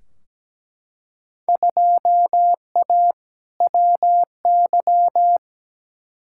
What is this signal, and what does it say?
Morse code, heard as a keyed tone, about 17 wpm, 700 Hz. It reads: 2A WY
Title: Field Day Exchanges - 17wpm